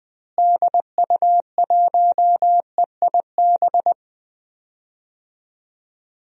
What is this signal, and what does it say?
Morse code, heard as a keyed tone, about 20 wpm, 700 Hz. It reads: DU1EIB
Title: DXCC-100 Call Signs - 6 Characters - Repeated 3x - 20wpm